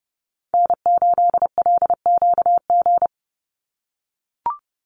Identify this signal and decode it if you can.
Morse code, heard as a keyed tone, about 30 wpm, 700 Hz. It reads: N8LQG
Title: USA Call Signs 30wpm